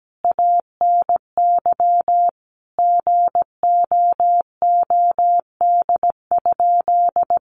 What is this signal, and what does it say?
Morse code, heard as a keyed tone, about 17 wpm, 700 Hz. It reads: ANY GOOD?
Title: Sentences from Top 1200 Words 17wpm